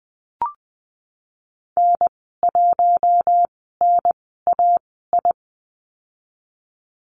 Morse code, heard as a keyed tone, about 19 wpm, spaced as for 14 wpm, 700 Hz. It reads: N1NAI
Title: 055 Lesson 10g - Letter r - Call Signs 20wpm